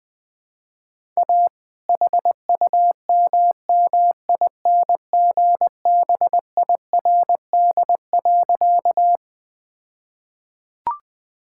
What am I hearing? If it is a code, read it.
Morse code, heard as a keyed tone, about 21 wpm, 700 Hz. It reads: A HUMMINGBIRD.
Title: All-Age Appropriate Jokes - 20wpm